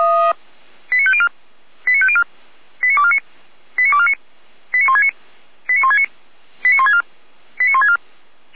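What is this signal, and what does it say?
Five-tone selective call (ZVEI), heard as quick runs of five tones, tones of about 70 ms. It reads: 89504 89504 89249 89249 89169 89169 89164 89164
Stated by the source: ZVEI five-tone selective call received via SDR.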